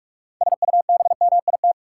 Morse code, heard as a keyed tone, about 46 wpm, 700 Hz. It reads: SUBMIT
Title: Six Letter Words - 45wpm